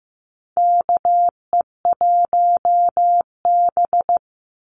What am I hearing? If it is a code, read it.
Morse code, heard as a keyed tone, about 15 wpm, 700 Hz. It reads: KE1B
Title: Call Signs Repeated 2x - 15wpm